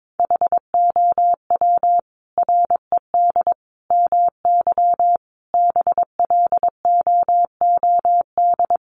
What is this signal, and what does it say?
Morse code, heard as a keyed tone, about 22 wpm, 700 Hz. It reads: HOW RED MY BLOOD
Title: Sentences from Top 1100 Words 22wpm